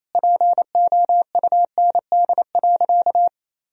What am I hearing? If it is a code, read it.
Morse code, heard as a keyed tone, about 27 wpm, 700 Hz. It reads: POUND.
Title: Sentences from Top 1000 Words - Review Part 2 - 28wpm